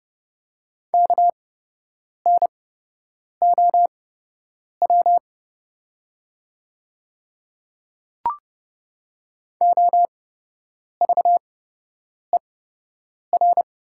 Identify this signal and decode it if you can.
Morse code, heard as a keyed tone, about 30 wpm, spaced as for 8 wpm, 700 Hz. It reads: KNOW OVER
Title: 8x Character Spacing - Top 100 Words - 30wpm